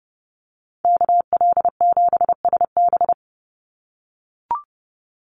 Morse code, heard as a keyed tone, about 30 wpm, 700 Hz. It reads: KL7SB
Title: Call Signs - 30wpm